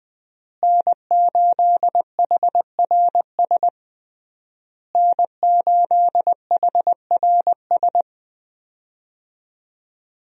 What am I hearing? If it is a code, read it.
Morse code, heard as a keyed tone, about 20 wpm, 700 Hz. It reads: N8HRS N8HRS
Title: USA Call Signs - 5 Characters - Repeated 3x - 20wpm